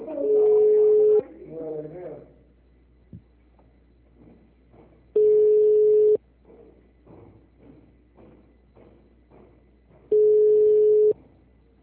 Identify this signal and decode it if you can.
Ringback tone (Europe, 425 Hz), with speech.